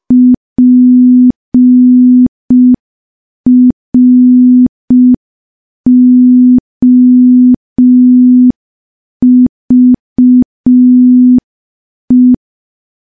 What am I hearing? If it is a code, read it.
Morse code, heard as a keyed tone, about 5 wpm, 250 Hz. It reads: PROVE
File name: morse / morse-training-05WPM